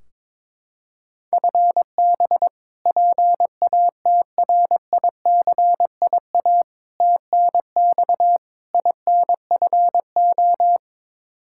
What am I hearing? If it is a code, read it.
Morse code, heard as a keyed tone, about 23 wpm, 700 Hz. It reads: FB PATRICIA TNX INFO
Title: Practice QSO - Exchange 3 - More Info - 22wpm